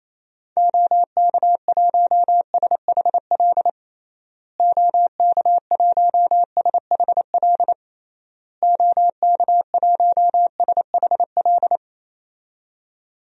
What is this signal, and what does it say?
Morse code, heard as a keyed tone, about 28 wpm, 700 Hz. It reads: OK1SHL OK1SHL OK1SHL
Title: DXCC-100 Call Signs - 6 Characters - Repeated 3x - 28wpm